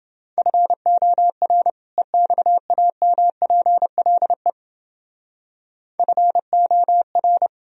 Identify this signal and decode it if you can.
Morse code, heard as a keyed tone, about 30 wpm, 700 Hz. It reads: FOR EXAMPLE FOR